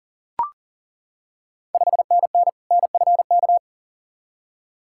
Morse code, heard as a keyed tone, about 39 wpm, 700 Hz. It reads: 5NN NFK
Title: Ontario QSO Party Exchange 40wpm